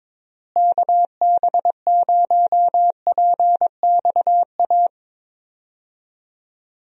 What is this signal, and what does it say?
Morse code, heard as a keyed tone, about 22 wpm, 700 Hz. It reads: KB0PXA
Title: USA Call Signs - 6 Characters - Repeated 3x - 22wpm